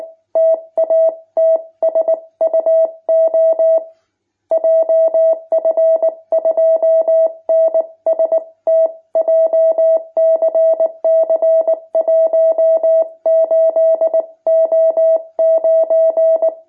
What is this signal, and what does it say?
Morse code, heard as a keyed tone, about 19 wpm, 650 Hz. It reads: TATSUO JF2NST JCC 18O9
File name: jh8cbh / 2021cw